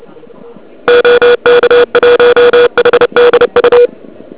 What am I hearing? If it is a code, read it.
Morse code, heard as a keyed tone, about 31 wpm, 2400 Hz. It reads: OK1HDU
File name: ARCHIV / KX3 / KV PA